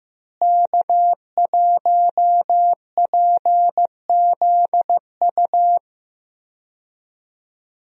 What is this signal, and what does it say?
Morse code, heard as a keyed tone, about 15 wpm, 700 Hz. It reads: K1PZU
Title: USA Call Signs - 5 Characters - Repeated 2x - 15wpm